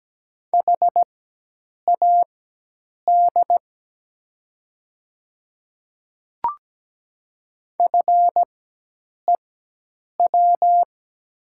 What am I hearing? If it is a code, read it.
Morse code, heard as a keyed tone, about 17 wpm, spaced as for 8 wpm, 700 Hz. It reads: HAD FEW
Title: 4x Character Spacing - Top 300 Words - 17wpm